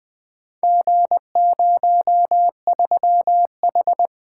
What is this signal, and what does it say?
Morse code, heard as a keyed tone, about 20 wpm, 700 Hz. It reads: G03H
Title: Four Letter-Number Combinations - 20wpm